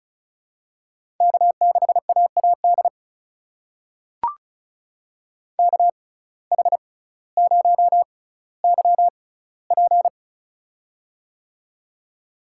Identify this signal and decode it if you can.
Morse code, heard as a keyed tone, about 36 wpm, spaced as for 18 wpm, 700 Hz. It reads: K6AAD KH0YP
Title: USA Call Signs - 5 Characters - 6x Character Spacing - 35wpm